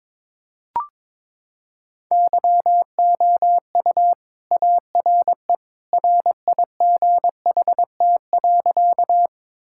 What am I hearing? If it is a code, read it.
Morse code, heard as a keyed tone, about 22 wpm, 700 Hz. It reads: YOU ARE RIGHT.
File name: practice / 22wpm / Sentences with 2-Word Phrases - 22wpm